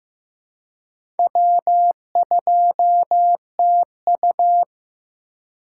Morse code, heard as a keyed tone, about 15 wpm, 700 Hz. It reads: W2TU
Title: USA Call Signs - 4x Character Spacing - 15wpm